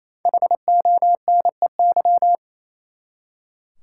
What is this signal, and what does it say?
Morse code, heard as a keyed tone, about 28 wpm, 700 Hz. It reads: HONEY
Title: Five Letter Words - 28wpm